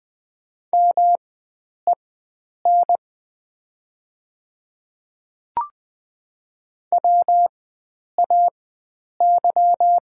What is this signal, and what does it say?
Morse code, heard as a keyed tone, about 20 wpm, spaced as for 9 wpm, 690 Hz. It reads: MEN WAY
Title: 4x Character Spacing - Top 300 Words - 20wpm